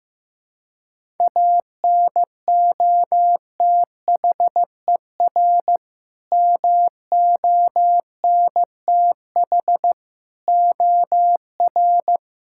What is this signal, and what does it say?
Morse code, heard as a keyed tone, about 15 wpm, 700 Hz. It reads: ANOTHER MONTH OR
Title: Sentences from Top 1000 Words - Review Part 5 - 15wpm